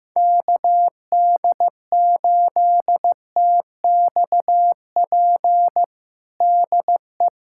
Morse code, heard as a keyed tone, about 15 wpm, 700 Hz. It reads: KD8TXP DE
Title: USA Callsign DE Callsign - 6 Characters - 15wpm